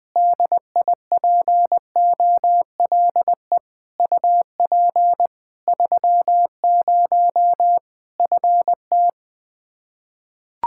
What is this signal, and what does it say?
Morse code, heard as a keyed tone, about 20 wpm, 700 Hz. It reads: DIPOLE UP 30 FT
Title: Practice QSO - Exchange 2 - Info - 20wpm